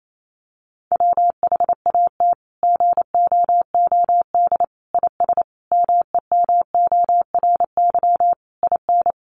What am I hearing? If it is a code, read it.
Morse code, heard as a keyed tone, about 28 wpm, 700 Hz. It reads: WHAT GOOD IS MEMORY IN
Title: Sentences from Top 1300 Words 28wpm